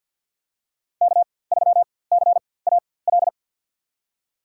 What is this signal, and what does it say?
Morse code, heard as a keyed tone, about 54 wpm, spaced as for 24 wpm, 700 Hz. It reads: K3CAL
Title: USA Call Signs - 5 Characters - 4x Character Spacing - 50wpm